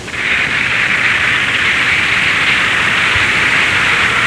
Emergency Alert System (SAME) tones, heard as harsh buzzing bursts. It is heard from 0.7 to 3.9 s.